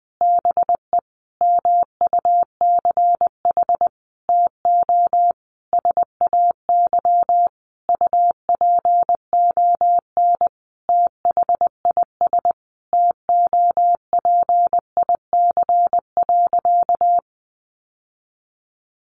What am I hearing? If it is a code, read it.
Morse code, heard as a keyed tone, about 20 wpm, 700 Hz. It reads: BE MUCH TO SAY UPON THIS TOPIC.
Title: Sentences from Top 1200 Words 20wpm